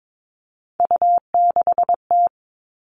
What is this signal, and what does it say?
Morse code, heard as a keyed tone, about 22 wpm, 700 Hz. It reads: U6T
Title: All Cut Numbers - 3-Digit 22wpm